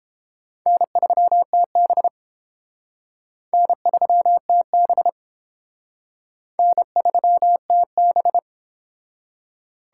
Morse code, heard as a keyed tone, about 32 wpm, 700 Hz. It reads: N3TB N3TB N3TB